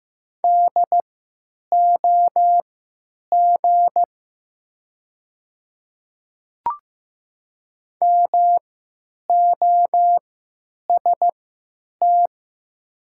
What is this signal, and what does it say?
Morse code, heard as a keyed tone, about 15 wpm, spaced as for 9 wpm, 700 Hz. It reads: DOG MOST
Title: 3x Character Spacing - Top 500 Words - 15wpm